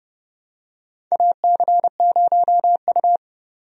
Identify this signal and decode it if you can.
Morse code, heard as a keyed tone, about 31 wpm, 700 Hz. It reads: AC0U
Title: USA Call Signs - 4 Characters - 30wpm